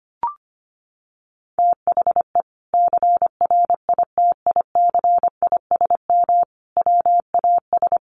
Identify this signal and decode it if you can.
Morse code, heard as a keyed tone, about 25 wpm, 700 Hz. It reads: THE CRITICISM WAS
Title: Sentences from Top 1300 Words 25wpm